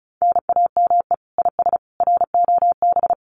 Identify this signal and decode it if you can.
Morse code, heard as a keyed tone, about 34 wpm, 700 Hz. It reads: NAME IS ROB
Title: Common QSO Phrases 35wpm